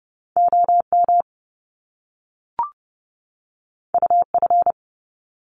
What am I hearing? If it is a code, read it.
Morse code, heard as a keyed tone, about 30 wpm, 700 Hz. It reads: OM UF
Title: Top Two Letter Combinations in Words 30wpm